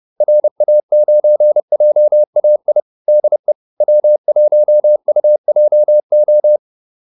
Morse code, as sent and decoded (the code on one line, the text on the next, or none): .-. .- ----. .--- .- .. / -.. / .-- .---- ..- .--- ---
RA9JAI DE W1UJO